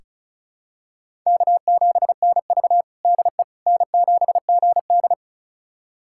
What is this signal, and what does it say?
Morse code, heard as a keyed tone, about 36 wpm, 700 Hz. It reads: K7NV DE N7GD
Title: USA Callsign DE Callsign - 4 Characters - 35wpm